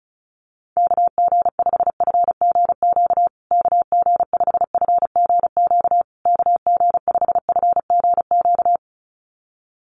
Morse code, heard as a keyed tone, about 34 wpm, 700 Hz. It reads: KG5FGQ KG5FGQ KG5FGQ
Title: USA Call Signs Repeated 3x 35wpm